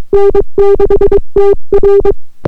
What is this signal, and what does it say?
Morse code, heard as a keyed tone, about 22 wpm, 400 Hz. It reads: N6TR